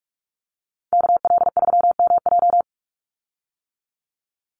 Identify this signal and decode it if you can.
Morse code, heard as a keyed tone, about 44 wpm, 700 Hz. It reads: KL3MJ
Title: USA Call Signs 45wpm